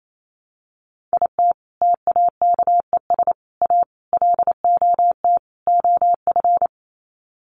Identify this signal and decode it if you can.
Morse code, heard as a keyed tone, about 28 wpm, 700 Hz. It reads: IT TAKES A LOT OF
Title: Sets of 5 Words Encore 28wpm